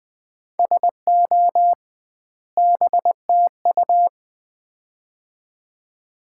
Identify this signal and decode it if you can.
Morse code, heard as a keyed tone, about 20 wpm, 700 Hz. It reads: SO BTU
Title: Common QSO Phrases - 2x Word-Spacing - 20wpm